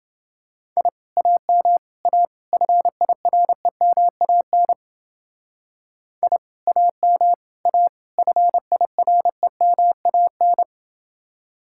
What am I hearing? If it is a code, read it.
Morse code, heard as a keyed tone, about 29 wpm, 700 Hz. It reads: I AM A FIREMAN I AM A FIREMAN